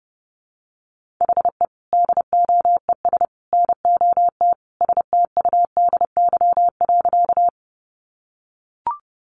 Morse code, heard as a keyed tone, about 30 wpm, 700 Hz. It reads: HE DOES NOT STUDY.
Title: Additional Easy Phrases 30wpm